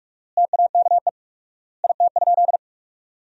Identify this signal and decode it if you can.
Morse code, heard as a keyed tone, about 46 wpm, 700 Hz. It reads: TAKE IT?
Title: 4x Word-Spacing - Sentences from Top 300 Words - 45wpm